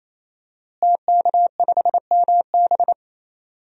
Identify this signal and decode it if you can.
Morse code, heard as a keyed tone, about 28 wpm, 700 Hz. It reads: TK5MB
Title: DXCC-100 Call Signs - 5 Characters - Repeated 3x - 28wpm